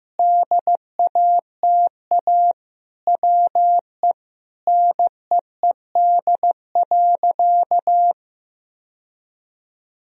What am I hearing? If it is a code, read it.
Morse code, heard as a keyed tone, about 15 wpm, 700 Hz. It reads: DATA WE NEED.
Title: Sentences from Top 1000 Words - Review Part 3 - 15wpm